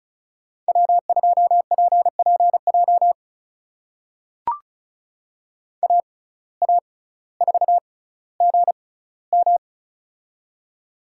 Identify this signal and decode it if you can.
Morse code, heard as a keyed tone, about 36 wpm, spaced as for 18 wpm, 700 Hz. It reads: W2PPJ AA4GM
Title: USA Call Signs - 5 Characters - 6x Character Spacing - 35wpm